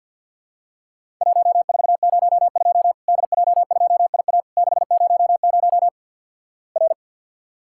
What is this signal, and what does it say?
Morse code, heard as a keyed tone, about 50 wpm, 700 Hz. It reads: 1402 DJ2IA 600 R
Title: Worked All Europe DX Contest - QTC Exchange with Repeats - 50wpm